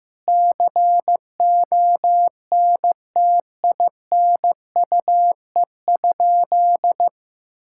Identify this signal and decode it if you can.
Morse code, heard as a keyed tone, about 15 wpm, 700 Hz. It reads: CONTINUE?
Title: Sentences from Top 1000 Words - Review Part 6 - 15wpm